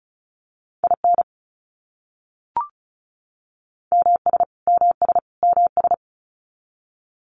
Morse code, heard as a keyed tone, about 34 wpm, 700 Hz. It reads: IN MS MS MS
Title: US States Repeated 3x 35wpm